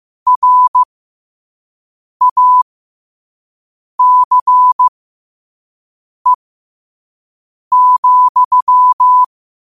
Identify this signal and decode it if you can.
Morse code, heard as a keyed tone, about 15 wpm, spaced as for 5 wpm, 1000 Hz. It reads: RACE,
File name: qotdfiles / 5wpm